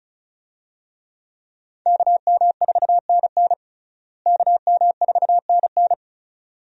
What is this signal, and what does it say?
Morse code, heard as a keyed tone, about 36 wpm, 700 Hz. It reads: KM4NN KM4NN
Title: USA Call Signs - 5 Characters - Repeated 2x - 35wpm